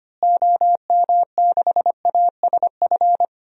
Morse code, heard as a keyed tone, about 25 wpm, 700 Hz. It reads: OM6ASF
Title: DXCC-100 Call Signs - 6 Characters - Repeated 3x - 25wpm